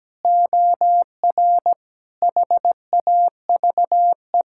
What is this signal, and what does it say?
Morse code, heard as a keyed tone, about 17 wpm, 700 Hz. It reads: OR HAVE
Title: Sentences from Top 1000 Words - Review Part 6 - 17wpm